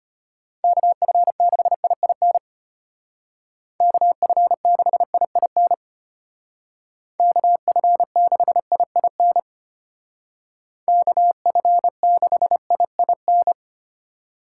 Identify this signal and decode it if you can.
Morse code, heard as a keyed tone, about 33 wpm, 700 Hz. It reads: KF6IIN KF6IIN KF6IIN KF6IIN